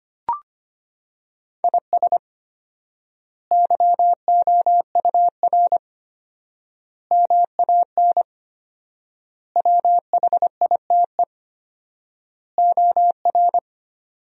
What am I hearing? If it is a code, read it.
Morse code, heard as a keyed tone, about 25 wpm, 700 Hz. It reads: IS YOUR MAN WHITE OR
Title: 4x Word-Spacing - Sentences from Top 400 Words - 25wpm